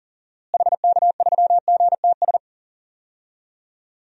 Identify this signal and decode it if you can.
Morse code, heard as a keyed tone, about 37 wpm, 700 Hz. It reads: HK3GTS
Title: DXCC-100 Call Signs - 6 Characters - Repeated 3x - 40wpm